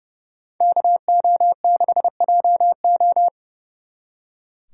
Morse code, heard as a keyed tone, about 39 wpm, 700 Hz. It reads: KO6JO